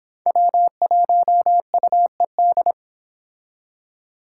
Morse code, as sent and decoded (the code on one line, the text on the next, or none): .-- .---- ..- . -..
W1UED